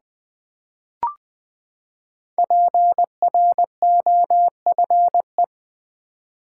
Morse code, heard as a keyed tone, about 20 wpm, 700 Hz. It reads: PROFE
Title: Build A Word - 5 Letters At A Time - 12-Letter Words - 20wpm